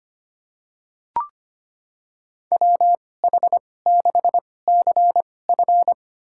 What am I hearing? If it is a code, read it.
Morse code, heard as a keyed tone, about 25 wpm, spaced as for 18 wpm, 700 Hz. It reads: WH6CF
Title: USA Call Signs - 5 Characters - 2x Character Spacing - 25wpm